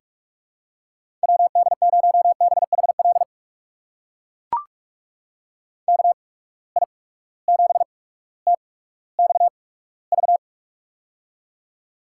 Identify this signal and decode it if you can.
Morse code, heard as a keyed tone, about 44 wpm, spaced as for 19 wpm, 700 Hz. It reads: WD0BHL KI7TXV
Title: USA Call Signs - 6 Characters - 8x Character Spacing - 45wpm